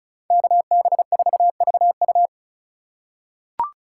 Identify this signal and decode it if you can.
Morse code, heard as a keyed tone, about 34 wpm, 700 Hz. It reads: KB4VU
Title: K1USN SST QSO Practice - 35wpm